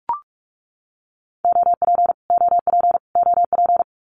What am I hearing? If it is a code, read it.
Morse code, heard as a keyed tone, about 44 wpm, 700 Hz. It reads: OP OP OP
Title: Abbreviations Repeated 3x - 45wpm